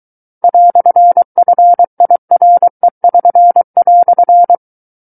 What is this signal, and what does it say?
Morse code, heard as a keyed tone, about 23 wpm, 700 Hz. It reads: "FIRE<SN>"
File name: Book One / 14-Chapter 14 - 23 WPM